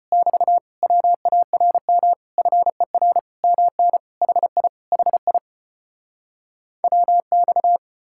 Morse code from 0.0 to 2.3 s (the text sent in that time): <BT> WARM